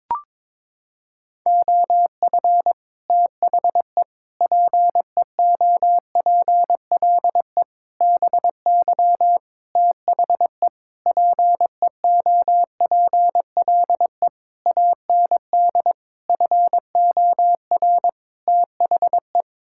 Morse code, heard as a keyed tone, about 22 wpm, 700 Hz. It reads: OF THE PEOPLE BY THE PEOPLE AND FOR THE
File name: Sets of 9 Words - 22wpm